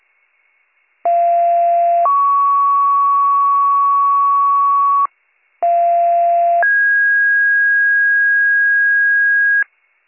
Two-tone page, heard as a one-second tone, then a three-second tone, about 690 then 1120 Hz, about 690 then 1650 Hz, with noise. It is heard from 1.1 to 9.6 s.